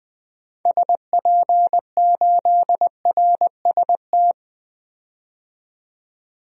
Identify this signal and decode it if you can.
Morse code, heard as a keyed tone, about 21 wpm, 700 Hz. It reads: SP8RST
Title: DXCC-100 Call Signs - 6 Characters - 20wpm